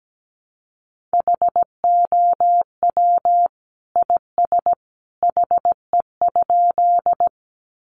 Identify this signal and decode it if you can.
Morse code, heard as a keyed tone, about 17 wpm, 700 Hz. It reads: HOW IS HE?
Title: Sentences from Top 100 Words 17wpm